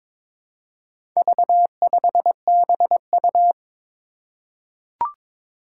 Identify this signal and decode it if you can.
Morse code, heard as a keyed tone, about 22 wpm, 700 Hz. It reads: V5BU
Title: Four Challenging Letter-Number Combinations 22wpm